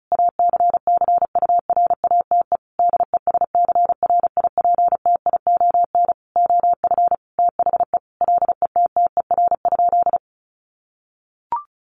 Morse code, heard as a keyed tone, about 34 wpm, 700 Hz. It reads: ACCURATE DESCRIPTION OF THE LETTER?
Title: Sentences from Top 1300 Words 35wpm